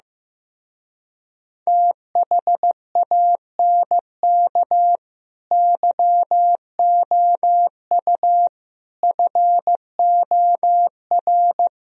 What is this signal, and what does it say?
Morse code, heard as a keyed tone, about 15 wpm, 700 Hz. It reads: THANK YOU FOR
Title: Speed-Racing - Sets of 5 Words 15wpm